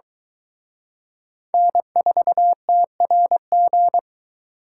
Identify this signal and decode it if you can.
Morse code, heard as a keyed tone, about 22 wpm, 700 Hz. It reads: N4TRG